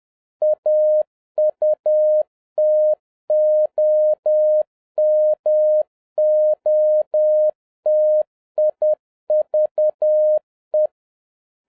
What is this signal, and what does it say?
Morse code, heard as a keyed tone, about 10 wpm, 610 Hz. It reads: AUTOMOTIVE
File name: 2011-07-mp3 / n56 BACK FROM A BIG ACCIDENT 10wpm